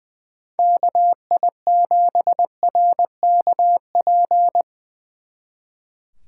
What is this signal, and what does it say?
Morse code, heard as a keyed tone, about 23 wpm, 700 Hz. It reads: KI7RKP
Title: Speed Racing - USA 2x3 Call Signs 20wpm